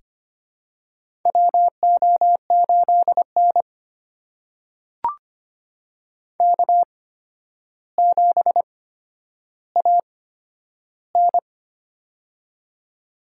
Morse code, heard as a keyed tone, about 25 wpm, spaced as for 11 wpm, 700 Hz. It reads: WO8N K7AN
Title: USA Call Signs - 4 Characters - 8x Character Spacing - 25wpm